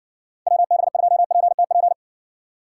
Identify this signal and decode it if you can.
Morse code, heard as a keyed tone, about 56 wpm, 700 Hz. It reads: WB2PTP